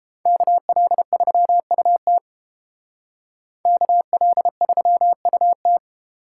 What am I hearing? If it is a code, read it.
Morse code, heard as a keyed tone, about 29 wpm, 700 Hz. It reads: KL3UT KL3UT